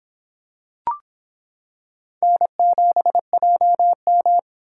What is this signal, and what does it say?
Morse code, heard as a keyed tone, about 25 wpm, 700 Hz. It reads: N7JM